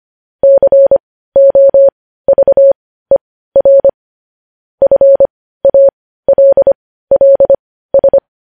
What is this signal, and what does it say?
Morse code, heard as a keyed tone, about 25 wpm, spaced as for 15 wpm, 550 Hz. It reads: COVER FALLS
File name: podcast / News headlines-1x-25x15